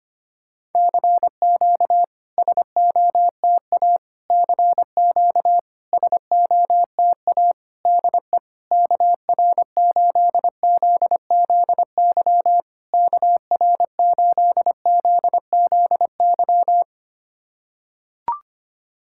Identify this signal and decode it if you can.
Morse code, heard as a keyed tone, about 26 wpm, 700 Hz. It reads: CQ SOTA CQ SOTA DE KR8ZZY KR8ZZY
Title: SOTA QSO Practice 25wpm